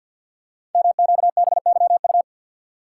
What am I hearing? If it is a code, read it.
Morse code, heard as a keyed tone, about 52 wpm, 690 Hz. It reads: MQBYU
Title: Code Groups - 5 Letters - 50wpm